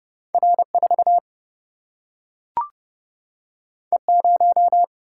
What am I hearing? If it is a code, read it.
Morse code, heard as a keyed tone, about 30 wpm, 700 Hz. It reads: R4 E0
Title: Single-Letter Single-Number 30wpm